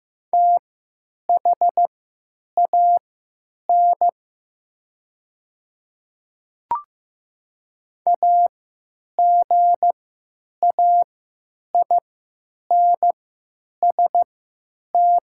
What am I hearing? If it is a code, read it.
Morse code, heard as a keyed tone, about 15 wpm, spaced as for 9 wpm, 700 Hz. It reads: THAN AGAINST
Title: 3x Character Spacing - Top 500 Words - 15wpm